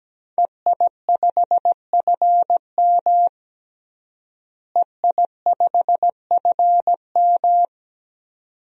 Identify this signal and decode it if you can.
Morse code, heard as a keyed tone, about 17 wpm, 700 Hz. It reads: EI5FM EI5FM
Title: DXCC-100 Call Signs - 5 Characters - Repeated 2x - 17wpm